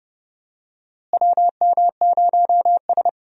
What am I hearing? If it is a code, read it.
Morse code, heard as a keyed tone, about 31 wpm, 700 Hz. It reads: WM0S
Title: USA Call Signs - 4 Characters - Repeated 3x - 30wpm